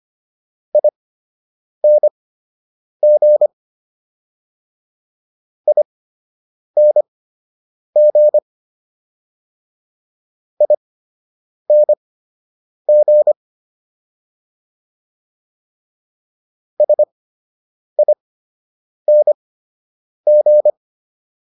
Morse code, heard as a keyed tone, about 25 wpm, spaced as for 8 wpm, 600 Hz. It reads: ING ING ING SING